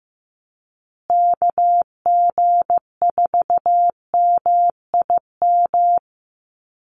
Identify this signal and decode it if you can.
Morse code, heard as a keyed tone, about 14 wpm, 700 Hz. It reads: KG4MIM
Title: USA Call Signs Repeated 3x 15wpm